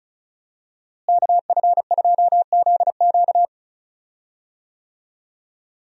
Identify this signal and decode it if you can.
Morse code, heard as a keyed tone, about 36 wpm, 700 Hz. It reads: KF2ZQ
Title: USA Call Signs - 5 Characters - Repeated 3x - 35wpm